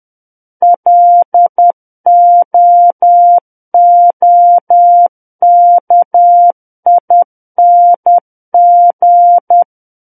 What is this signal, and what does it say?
Morse code, heard as a keyed tone, about 10 wpm, 700 Hz. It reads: LOOKING
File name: Book One / 01-Chapter 1 - 10 WPM